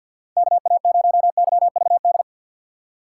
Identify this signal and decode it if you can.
Morse code, heard as a keyed tone, about 52 wpm, 700 Hz. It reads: KA0YVD
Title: USA Call Signs - 6 Characters - 6x Character Spacing - 50wpm